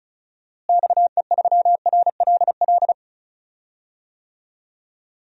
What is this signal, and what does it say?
Morse code, heard as a keyed tone, about 35 wpm, 700 Hz. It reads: XE3RLL
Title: DXCC-100 Call Signs - 6 Characters - Repeated 3x - 35wpm